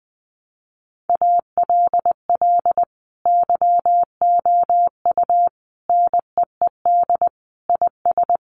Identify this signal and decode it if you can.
Morse code, heard as a keyed tone, about 20 wpm, 700 Hz. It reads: ALL YOU NEED IS
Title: Sentences from Top 1500 Words 20wpm